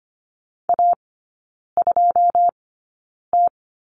Morse code, heard as a keyed tone, about 25 wpm, spaced as for 9 wpm, 690 Hz. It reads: A2T
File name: Lesson 14e - Number 2 - Three Characters 25wpm